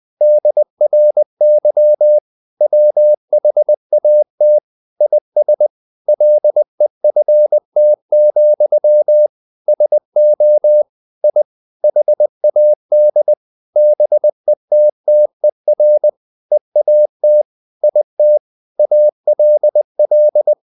Morse code, heard as a keyed tone, about 20 wpm, 600 Hz. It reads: DRY WHAT IS LEFT, SO I HAD BETTER EAT IT ALL
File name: the old man and the sea / the old man and the sea part 1 20wpm